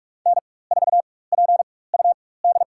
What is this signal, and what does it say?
Morse code, heard as a keyed tone, about 46 wpm, spaced as for 21 wpm, 700 Hz. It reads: N4PUD
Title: USA Call Signs - 5 Characters - 4x Character Spacing - 45wpm